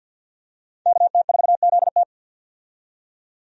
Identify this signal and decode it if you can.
Morse code, heard as a keyed tone, about 50 wpm, 700 Hz. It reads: KT4ZT
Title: USA Call Signs - 5 Characters - 50wpm